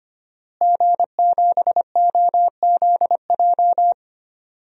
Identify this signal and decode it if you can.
Morse code, heard as a keyed tone, about 25 wpm, 700 Hz. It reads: G7OZJ
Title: DXCC-100 Call Signs - 5 Characters - Repeated 3x - 25wpm